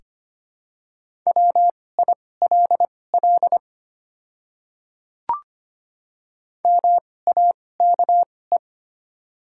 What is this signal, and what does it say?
Morse code, heard as a keyed tone, about 25 wpm, spaced as for 18 wpm, 700 Hz. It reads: WILL MAKE
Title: 2x Character Spacing - Top 200 Words - 25wpm